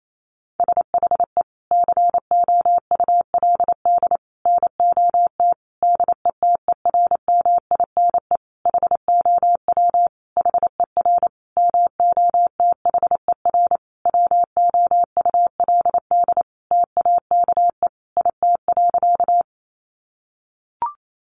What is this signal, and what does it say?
Morse code, heard as a keyed tone, about 28 wpm, 700 Hz. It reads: SHE COULD NOT DETERMINE HOW HER MOTHER WOULD TAKE IT.
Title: Sentences from Top 700 Words 28wpm